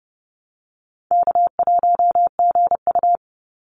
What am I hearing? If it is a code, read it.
Morse code, heard as a keyed tone, about 30 wpm, 700 Hz. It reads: K1GU
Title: Easy Call Signs Repeated 3x - 30wpm